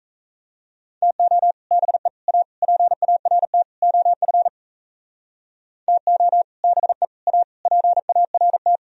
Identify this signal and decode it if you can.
Morse code, heard as a keyed tone, about 37 wpm, 700 Hz. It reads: TO BE A PART OF TO BE A PART